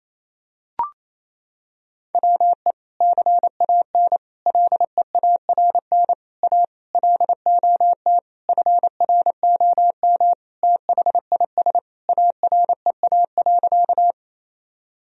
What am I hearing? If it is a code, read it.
Morse code, heard as a keyed tone, about 28 wpm, 700 Hz. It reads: WE CAN LEARN A LOT FROM THIS AREA.
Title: Sentences from Top 1000 Words - Review Part 2 - 28wpm